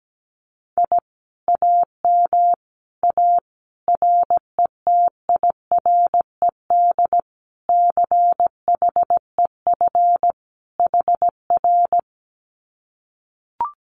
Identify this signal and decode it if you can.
Morse code, heard as a keyed tone, about 17 wpm, 700 Hz. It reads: I AM A RETIRED CHEF HR
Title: Common QSO Phrases Encore 17wpm